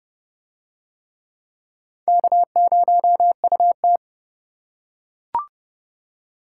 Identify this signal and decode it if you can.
Morse code, heard as a keyed tone, about 28 wpm, 700 Hz. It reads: K0UT